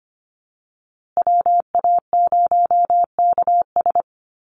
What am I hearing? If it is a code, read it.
Morse code, heard as a keyed tone, about 25 wpm, 700 Hz. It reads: WA0KS
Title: USA Call Signs Repeated 3x 25wpm